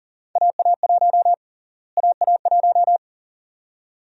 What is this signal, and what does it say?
Morse code, heard as a keyed tone, about 40 wpm, 700 Hz. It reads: AA1 AA1
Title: USA Call Sign Prefixes - Repeated 2x - 40wpm